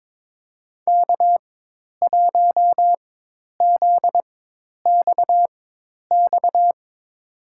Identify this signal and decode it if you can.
Morse code, heard as a keyed tone, about 22 wpm, spaced as for 10 wpm, 700 Hz. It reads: K1ZXX
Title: USA Call Signs - 5 Characters - 4x Character Spacing - 22wpm